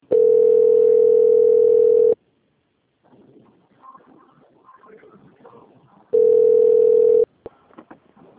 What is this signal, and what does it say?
Ringback tone (North America), 440 + 480 Hz.
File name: out- - - - - .5925